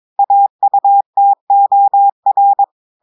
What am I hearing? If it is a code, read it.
Morse code, heard as a keyed tone, about 22 wpm, 800 Hz. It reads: AUTOR
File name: audios / portugais / portugais texte en clair 1 vitesse 22 mots minutes